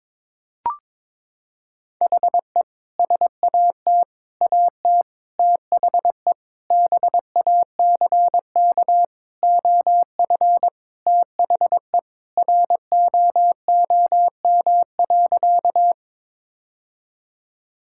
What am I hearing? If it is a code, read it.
Morse code, heard as a keyed tone, about 22 wpm, 700 Hz. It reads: HE SAT AT THE BACK OF THE ROOM.
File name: practice / 22wpm / Sentences with 3-Word Phrases - 22wpm